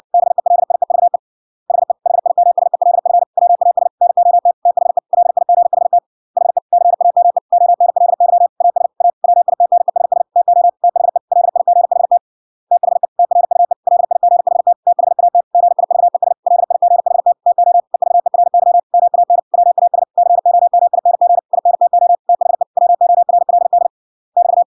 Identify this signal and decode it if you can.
Morse code, heard as a keyed tone, about 56 wpm, 700 Hz. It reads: BELIEVE HE HIMSELF WAS NOT THE LIGHT HE CAME ONLY AS A WITNESS TO THE LIGHT THE TRUE LIGHT THAT GIVES LIGHT TO EVERY MAN WAS COMING INTO THE WORLD <BT>